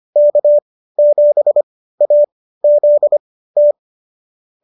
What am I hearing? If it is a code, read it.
Morse code, heard as a keyed tone, about 25 wpm, spaced as for 15 wpm, 570 Hz. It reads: K7AZT